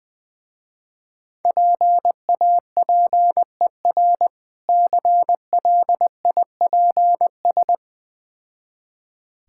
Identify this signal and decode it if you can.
Morse code, heard as a keyed tone, about 20 wpm, 700 Hz. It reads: PAPER CLIPS
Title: Difficult Words with Hints - 20wpm